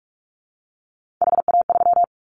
Morse code, heard as a keyed tone, about 44 wpm, 700 Hz. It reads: HA3
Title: Callsign Prefixes for Top 50 DXCC - 45wpm